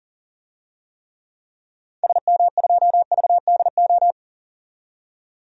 Morse code, heard as a keyed tone, about 38 wpm, 700 Hz. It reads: SM2VDO
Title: DXCC-100 Call Signs - 6 Characters - 40wpm